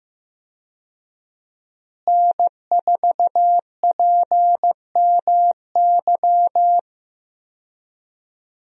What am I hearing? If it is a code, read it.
Morse code, heard as a keyed tone, about 15 wpm, 700 Hz. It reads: N4PMY
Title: USA Call Signs - 5 Characters - Repeated 2x - 15wpm